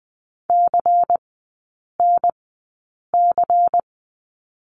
Morse code, heard as a keyed tone, about 20 wpm, spaced as for 8 wpm, 700 Hz. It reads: CNC
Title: 127 Lesson 21e - Number 6 - Three Characters 20wpm